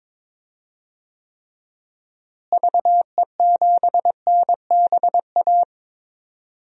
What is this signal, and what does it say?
Morse code, heard as a keyed tone, about 21 wpm, 700 Hz. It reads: VE7NBA
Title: DXCC-100 Call Signs - 6 Characters - Repeated 2x - 22wpm